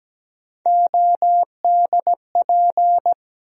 Morse code, heard as a keyed tone, about 17 wpm, 700 Hz. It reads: ODP
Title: USA Call Sign Postfixes - 3 Letters 17wpm